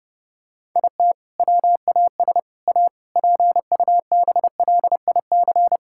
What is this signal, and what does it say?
Morse code, heard as a keyed tone, about 30 wpm, 700 Hz. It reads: IT WAS A PUBLIC
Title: Sentences from Top 1600 Words 30wpm